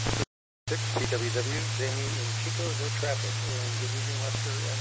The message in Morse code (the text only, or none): none